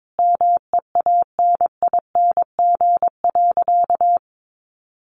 Morse code, heard as a keyed tone, about 22 wpm, 700 Hz. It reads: MEANING.
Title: Sentences from Top 1100 Words 22wpm